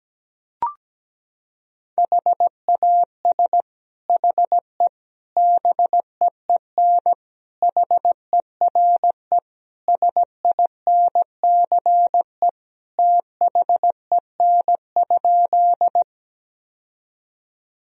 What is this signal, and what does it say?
Morse code, heard as a keyed tone, about 17 wpm, 700 Hz. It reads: HAS HE BEEN HERE SINCE THEN?
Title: Sentences from Top 1000 Words - Review Part 1 - 17wpm